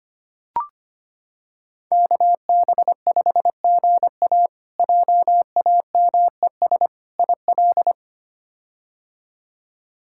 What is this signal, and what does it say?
Morse code, heard as a keyed tone, about 25 wpm, 700 Hz. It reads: KB5GA JAMES IL
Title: K1USN SST QSO Practice - 25wpm